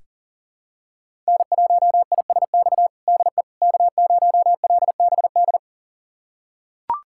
Morse code, heard as a keyed tone, about 40 wpm, 700 Hz. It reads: N1ISX DE K0LBD
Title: USA Callsign DE Callsign - 5 Characters - 40wpm